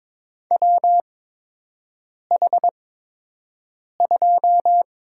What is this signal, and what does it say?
Morse code, heard as a keyed tone, about 22 wpm, spaced as for 6 wpm, 700 Hz. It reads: WH2
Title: USA Call Signs - 5 Characters - 8x Character Spacing - 22wpm